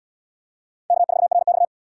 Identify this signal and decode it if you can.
Morse code, heard as a keyed tone, about 69 wpm, 700 Hz. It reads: K4RX